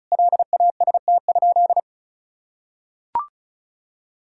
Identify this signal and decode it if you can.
Morse code, heard as a keyed tone, about 36 wpm, 700 Hz. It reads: LAST?
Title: Sentences from Top 1000 Words - Review Part 4 - 35wpm